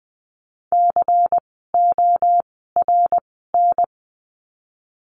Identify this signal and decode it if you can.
Morse code, heard as a keyed tone, about 20 wpm, spaced as for 15 wpm, 700 Hz. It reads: CORN
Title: 094 Lesson 16f - Letter c - Words 20wpm